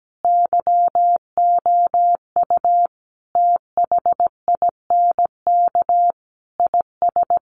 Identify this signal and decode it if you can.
Morse code, heard as a keyed tone, about 17 wpm, 700 Hz. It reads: YOU THINK IS
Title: Sets of 5 Words Encore 17wpm